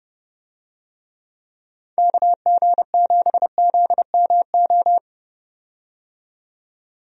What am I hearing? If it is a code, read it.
Morse code, heard as a keyed tone, about 30 wpm, 700 Hz. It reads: KG7ZMO
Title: USA Call Signs - 6 Characters - 30wpm